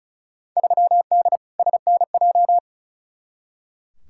Morse code, heard as a keyed tone, about 36 wpm, 700 Hz. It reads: 3D SNJ
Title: Field Day Exchanges Encore - 35wpm